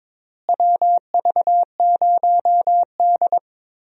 Morse code, heard as a keyed tone, about 22 wpm, 700 Hz. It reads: WV0D
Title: USA Call Signs - 4 Characters - Repeated 2x - 22wpm